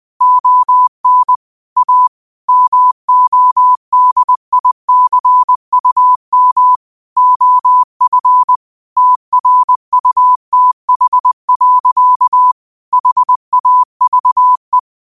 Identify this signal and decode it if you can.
Morse code, heard as a keyed tone, about 20 wpm, 1000 Hz. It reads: ON A MODICUM OF TRUTH. HAVE
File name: qotdfiles / 20wpm